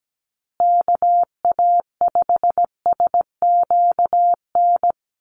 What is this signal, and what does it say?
Morse code, heard as a keyed tone, about 17 wpm, 700 Hz. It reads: KA5SQN
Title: USA Call Signs Repeated 3x 17wpm